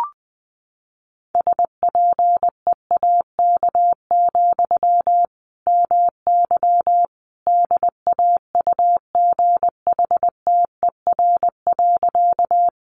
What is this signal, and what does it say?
Morse code, heard as a keyed tone, about 20 wpm, 700 Hz. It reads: SPEAK, MY DAUGHTER.
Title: Sentences from Top 1500 Words 20wpm